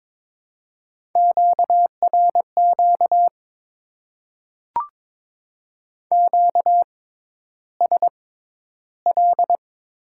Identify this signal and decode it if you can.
Morse code, heard as a keyed tone, about 22 wpm, spaced as for 11 wpm, 700 Hz. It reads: QRQ QSL
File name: Top 90 QSO Elements - 6x Character Spacing - 22wpm